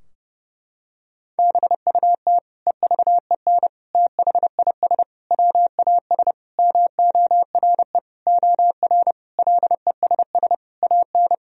Morse code, heard as a keyed tone, about 32 wpm, 700 Hz. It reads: BUT EVEN THIS WAS MORE OR LESS AN
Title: Sentences from Top 1000 Words - Review Part 2 - 30wpm